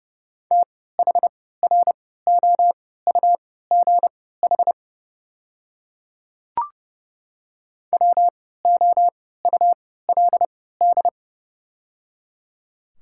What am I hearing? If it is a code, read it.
Morse code, heard as a keyed tone, about 30 wpm, spaced as for 17 wpm, 700 Hz. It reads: THROUGH WOULD
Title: 3x Character Spacing - Top 200 Words - 30wpm